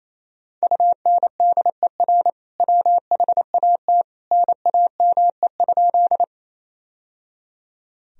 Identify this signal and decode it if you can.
Morse code, heard as a keyed tone, about 28 wpm, 700 Hz. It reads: UNDER WHAT NAME?
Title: Sentences from Top 1000 Words - Review Part 1 - 28wpm